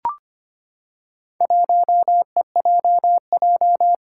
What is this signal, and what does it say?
Morse code, heard as a keyed tone, about 25 wpm, 700 Hz. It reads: 1EJJ
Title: Four Challenging Letter-Number Combinations 25wpm